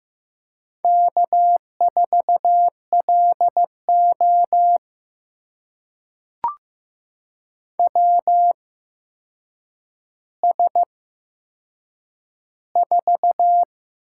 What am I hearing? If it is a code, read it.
Morse code, heard as a keyed tone, about 15 wpm, spaced as for 7 wpm, 700 Hz. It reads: K4LO WS4
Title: USA Call Signs - 4 Characters - 8x Character Spacing - 15wpm